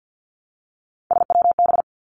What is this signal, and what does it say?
Morse code, heard as a keyed tone, about 50 wpm, 700 Hz. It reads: SWB
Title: Three Challenging Letter-Number Combinations 50wpm